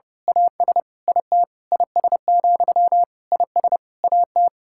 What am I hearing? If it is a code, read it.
Morse code, heard as a keyed tone, about 30 wpm, 700 Hz. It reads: AS IT IS, IS AT
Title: Sentences from Top 1600 Words 30wpm